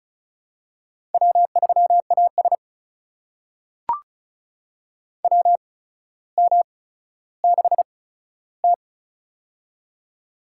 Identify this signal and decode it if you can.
Morse code, heard as a keyed tone, about 36 wpm, spaced as for 15 wpm, 700 Hz. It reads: W3AS WM6T
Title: USA Call Signs - 4 Characters - 8x Character Spacing - 35wpm